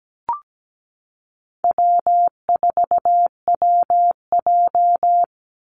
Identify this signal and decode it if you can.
Morse code, heard as a keyed tone, about 17 wpm, 700 Hz. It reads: W4WJ
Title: Easy Call Signs - 17wpm